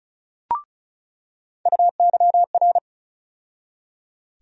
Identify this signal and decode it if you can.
Morse code, heard as a keyed tone, about 36 wpm, 700 Hz. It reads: UYR
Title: USA Call Sign Postfixes - 3 Letters 35wpm